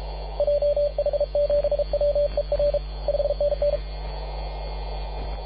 Morse code, heard as a keyed tone, about 32 wpm, 580 Hz. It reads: JH7WER 5NN
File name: posts / 2015-sept-24-2118z-t2gc-18-071-cw